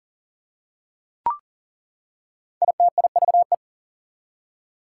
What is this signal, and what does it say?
Morse code, heard as a keyed tone, about 40 wpm, 700 Hz. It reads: ITIVE
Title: Prefixes and Suffixes 40wpm